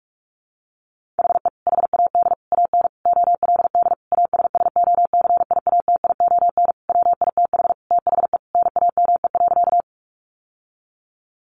Morse code, heard as a keyed tone, about 44 wpm, 700 Hz. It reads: HE HAD AN OLD ASSOCIATION WITH THE NAME.
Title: Sentences from Top 1100 Words 45wpm